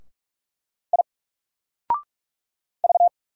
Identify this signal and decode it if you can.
Morse code, heard as a keyed tone, about 42 wpm, 700 Hz. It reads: I V
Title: Single Letter-Number 45wpm